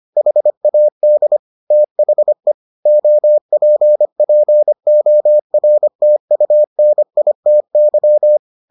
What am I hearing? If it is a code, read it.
Morse code, heard as a keyed tone, about 25 wpm, 600 Hz. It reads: HAD THE OPPORTUNITY